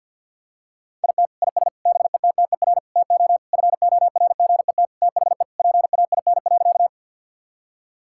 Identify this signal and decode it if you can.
Morse code, heard as a keyed tone, about 50 wpm, 700 Hz. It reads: IT IS BETTER TO FORGET THE PAIN.
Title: Sentences from Top 1600 Words 50wpm